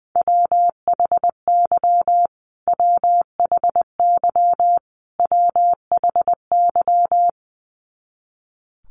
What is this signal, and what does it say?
Morse code, heard as a keyed tone, about 20 wpm, 700 Hz. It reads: WHY WHY WHY
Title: Top 500 Words Repeated 20wpm